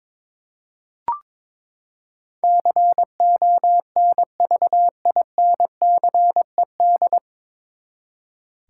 Morse code, heard as a keyed tone, about 22 wpm, 700 Hz. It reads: CONVINCED
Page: Morse Code Ninja practice 22wpm